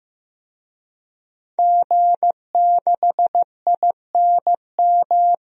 Morse code, heard as a keyed tone, about 15 wpm, 700 Hz. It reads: G6INM
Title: DXCC-100 Call Signs - 5 Characters - Repeated 2x - 15wpm